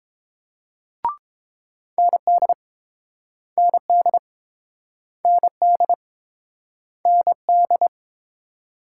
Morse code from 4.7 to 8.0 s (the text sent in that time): ND ND